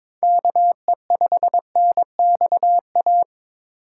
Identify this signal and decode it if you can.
Morse code, heard as a keyed tone, about 22 wpm, 700 Hz. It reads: KE5NXA
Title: USA Call Signs - 2x Character Spacing - 22wpm